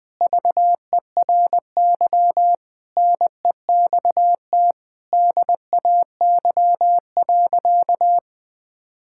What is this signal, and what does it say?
Morse code, heard as a keyed tone, about 19 wpm, 700 Hz. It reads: VERY NEXT DAY.
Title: Sentences from Top 1000 Words - Review Part 6 - 20wpm